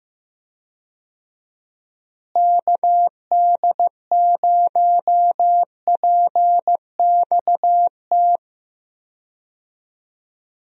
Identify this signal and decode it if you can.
Morse code, heard as a keyed tone, about 15 wpm, 700 Hz. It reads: KD0PXT
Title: USA Call Signs - 6 Characters - Repeated 2x - 15wpm